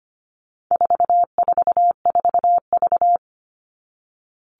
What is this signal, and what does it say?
Morse code, heard as a keyed tone, about 25 wpm, 700 Hz. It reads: 444V
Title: All Cut Numbers - 4-Digit 25wpm